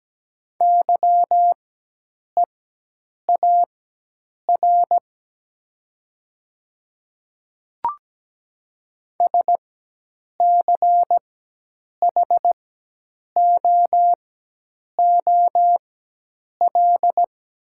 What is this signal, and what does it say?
Morse code, heard as a keyed tone, about 17 wpm, spaced as for 8 wpm, 700 Hz. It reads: YEAR SCHOOL
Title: 4x Character Spacing - Top 300 Words - 17wpm